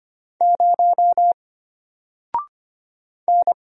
Morse code, heard as a keyed tone, about 25 wpm, 700 Hz. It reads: 0 N
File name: Single-Letter-Number-25wpm